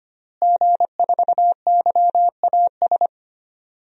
Morse code, heard as a keyed tone, about 25 wpm, 700 Hz. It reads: G4YAS
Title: DXCC-100 Call Signs - 5 Characters - Repeated 2x - 25wpm